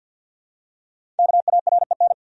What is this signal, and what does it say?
Morse code, heard as a keyed tone, about 51 wpm, 700 Hz. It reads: KAREN
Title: USA Common Names - 50wpm